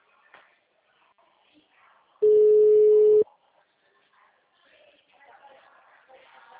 Ringback tone (Europe, 425 Hz).